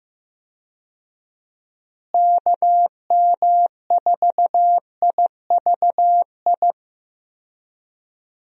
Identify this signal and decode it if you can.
Morse code, heard as a keyed tone, about 15 wpm, 700 Hz. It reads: KM4IVI
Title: USA Call Signs - 6 Characters - Repeated 2x - 15wpm